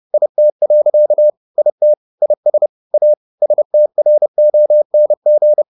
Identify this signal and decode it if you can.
Morse code, heard as a keyed tone, about 30 wpm, 600 Hz. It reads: IT. IT IS A STRONG
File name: the old man and the sea / the old man and the sea 1 30wpm